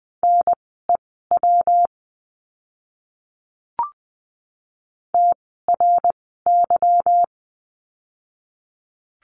Morse code, heard as a keyed tone, about 20 wpm, spaced as for 14 wpm, 700 Hz. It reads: NEW TRY
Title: 192 Lesson 31f - Letter v - Words 20wpm